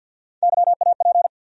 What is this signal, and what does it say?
Morse code, heard as a keyed tone, about 47 wpm, 700 Hz. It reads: YAP
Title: USA Call Sign Postfixes - 3 Letters 50wpm